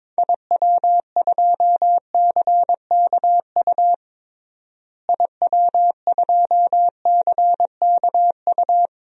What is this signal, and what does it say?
Morse code, heard as a keyed tone, about 22 wpm, 700 Hz. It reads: IW2CKU IW2CKU
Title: DXCC-100 Call Signs - 6 Characters - Repeated 2x - 22wpm